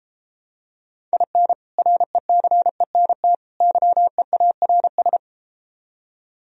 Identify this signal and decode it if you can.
Morse code, heard as a keyed tone, about 33 wpm, 700 Hz. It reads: IN RECENT YEARS